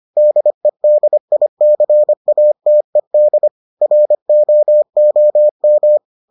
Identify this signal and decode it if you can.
Morse code, heard as a keyed tone, about 25 wpm, 600 Hz. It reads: DEDICATED ROOM